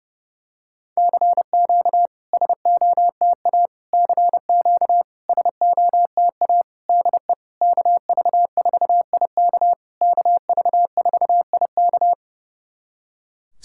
Morse code, heard as a keyed tone, about 30 wpm, 700 Hz. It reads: CQ SOTA CQ SOTA DE KV4IK KV4IK
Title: SOTA QSO Practice 30wpm